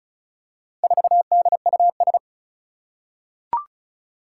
Morse code, heard as a keyed tone, about 36 wpm, 700 Hz. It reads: 4DUS
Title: Four Challenging Letter-Number Combinations 35wpm